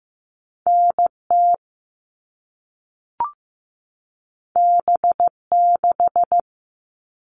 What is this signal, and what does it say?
Morse code, heard as a keyed tone, about 15 wpm, 700 Hz. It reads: NT B6
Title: All Cut Numbers - 2-Digit 15wpm